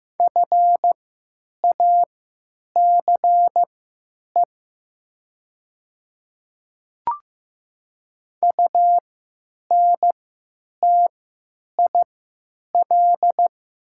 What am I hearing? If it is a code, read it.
Morse code, heard as a keyed tone, about 15 wpm, spaced as for 8 wpm, 700 Hz. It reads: FACE UNTIL
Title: 3x Character Spacing - Top 300 Words - 15wpm